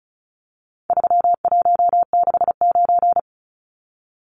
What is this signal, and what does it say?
Morse code, heard as a keyed tone, about 34 wpm, 700 Hz. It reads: 3169
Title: Numbers - 4-Digit 35wpm